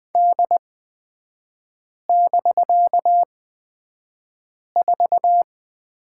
Morse code, heard as a keyed tone, about 20 wpm, spaced as for 5 wpm, 700 Hz. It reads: D<BK>4
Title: 255 Lesson 41e - Period - Three Characters 20wpm